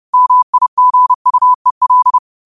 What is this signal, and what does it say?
Morse code, heard as a keyed tone, about 30 wpm, 1000 Hz. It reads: MIGUEL
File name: qotdfiles / 30wpm